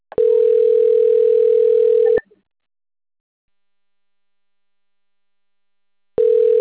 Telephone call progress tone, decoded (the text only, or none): ringback tone